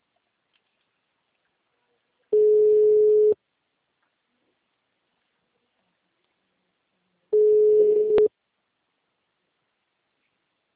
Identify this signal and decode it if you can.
Ringback tone (Europe, 425 Hz).